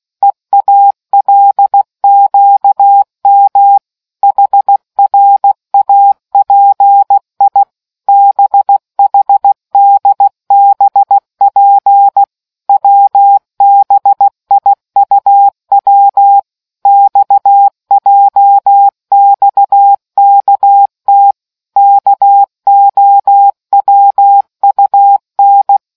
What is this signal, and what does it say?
Morse code, heard as a keyed tone, about 16 wpm, 800 Hz. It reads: EALQM HRAPI BHDBP WBIUW XJXKT KOWUN